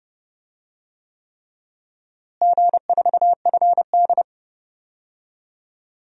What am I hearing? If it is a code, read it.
Morse code, heard as a keyed tone, about 30 wpm, 700 Hz. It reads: G4FD
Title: DXCC-100 Call Signs - 4 Characters - Repeated 3x - 30wpm